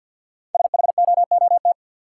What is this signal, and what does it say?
Morse code, heard as a keyed tone, about 46 wpm, 700 Hz. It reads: SHOOT
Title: Sentences from Top 1200 Words - New Word Drill Alt - 50wpm